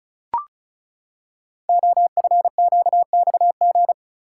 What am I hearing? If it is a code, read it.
Morse code, heard as a keyed tone, about 35 wpm, 700 Hz. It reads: OFQXG
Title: Code Groups - 5 Letters - 35wpm